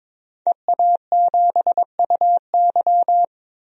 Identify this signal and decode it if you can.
Morse code, heard as a keyed tone, about 22 wpm, 700 Hz. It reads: EA7UY
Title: DXCC-100 Call Signs - 5 Characters - 22wpm